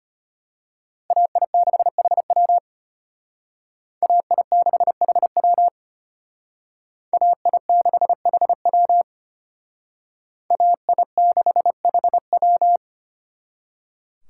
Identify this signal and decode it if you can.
Morse code, heard as a keyed tone, about 32 wpm, 700 Hz. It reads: AI6HW AI6HW AI6HW AI6HW